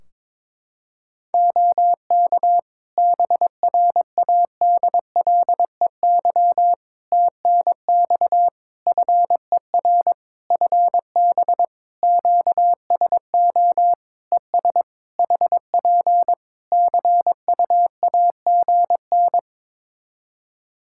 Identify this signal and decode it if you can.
Morse code, heard as a keyed tone, about 22 wpm, 700 Hz. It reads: OK BRADLEY TNX FER FB QSO ES HP CUAGN
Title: Practice QSO - Exchange 4 - Ending - 22wpm